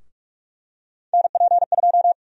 Morse code, heard as a keyed tone, about 48 wpm, 700 Hz. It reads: NP2
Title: USA Call Sign Prefixes - 45wpm